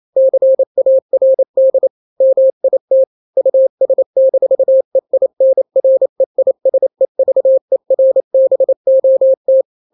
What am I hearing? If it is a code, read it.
Morse code, heard as a keyed tone, about 28 wpm, 530 Hz. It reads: CARD MIT US-EINREISEVERBOT
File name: daily / 27wpm